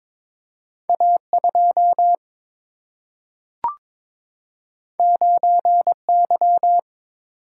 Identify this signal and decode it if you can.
Morse code, heard as a keyed tone, about 22 wpm, 700 Hz. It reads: A 2 9 Y
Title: Single-Letter Single-Number 22wpm